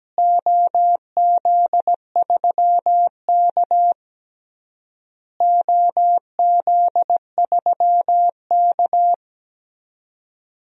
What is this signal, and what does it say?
Morse code, heard as a keyed tone, about 17 wpm, 700 Hz. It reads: OZ3K OZ3K
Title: DXCC-100 Call Signs - 4 Characters - Repeated 3x - 17wpm